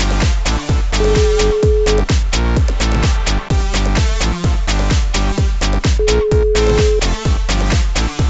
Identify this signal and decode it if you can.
Ringback tone (Europe, 425 Hz), with noise.